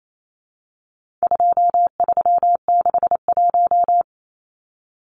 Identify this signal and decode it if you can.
Morse code, heard as a keyed tone, about 28 wpm, 700 Hz. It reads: 2361
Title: Numbers - 4-Digit 28wpm